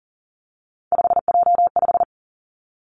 Morse code, heard as a keyed tone, about 37 wpm, 700 Hz. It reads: 5J5
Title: Three Challenging Letter-Number Combinations 40wpm